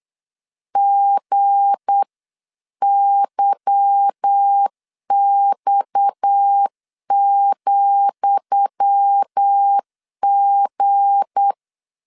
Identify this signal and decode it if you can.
Morse code, heard as a keyed tone, about 8 wpm, 800 Hz. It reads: GYX,G
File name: LEZIONI / 10 SIGNS / 10D COMA 8wpm 3749